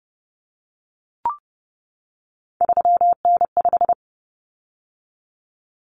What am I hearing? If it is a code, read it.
Morse code, heard as a keyed tone, about 30 wpm, 700 Hz. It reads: 3N5
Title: Common Cut Numbers - 3-Digit 30wpm